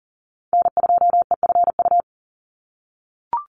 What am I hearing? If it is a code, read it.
Morse code, heard as a keyed tone, about 40 wpm, 690 Hz. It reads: N2EFU
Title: USA Call Signs 40wpm